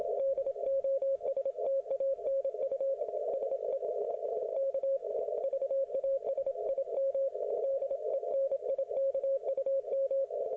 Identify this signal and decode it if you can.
Morse code, heard as a keyed tone, about 28 wpm, 540 Hz. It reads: DOSTANU SE K VASIM DENIKUM